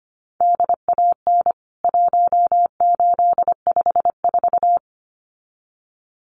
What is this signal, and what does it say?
Morse code, heard as a keyed tone, about 25 wpm, 700 Hz. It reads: DAN 1854
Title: CWT Exchange 25wpm